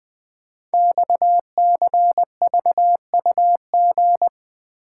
Morse code, heard as a keyed tone, about 20 wpm, 700 Hz. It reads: XCVUG
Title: Code Groups - 5 Letter:Numbers - 20wpm